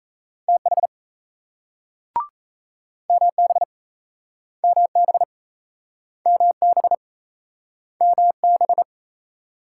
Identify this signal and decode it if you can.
Morse code, heard as a keyed tone, about 38 wpm, 700 Hz. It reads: TH MB MB MB MB